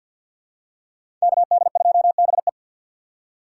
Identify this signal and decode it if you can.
Morse code, heard as a keyed tone, about 52 wpm, 700 Hz. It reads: KD2BE
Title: USA Call Signs - 5 Characters - 4x Character Spacing - 50wpm